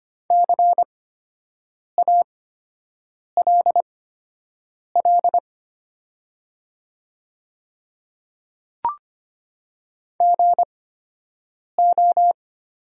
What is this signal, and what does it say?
Morse code, heard as a keyed tone, about 25 wpm, spaced as for 7 wpm, 700 Hz. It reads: CALL GO
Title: 8x Character Spacing - Top 100 Words - 25wpm